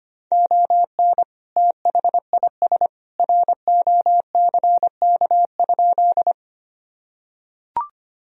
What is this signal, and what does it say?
Morse code, heard as a keyed tone, about 25 wpm, 700 Hz. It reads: ON THIS ROCK?
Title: Sentences from Top 1000 Words - Review Part 2 - 25wpm